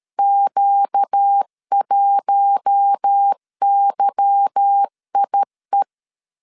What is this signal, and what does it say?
Morse code, heard as a keyed tone, about 13 wpm, 800 Hz. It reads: Q1YIE
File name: LEZIONI / 8 YJ45 / 8 Y 12wpm 2228